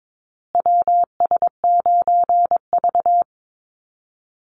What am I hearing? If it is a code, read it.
Morse code, heard as a keyed tone, about 22 wpm, 690 Hz. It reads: WS9V
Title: Call Signs Repeated 3x - 22wpm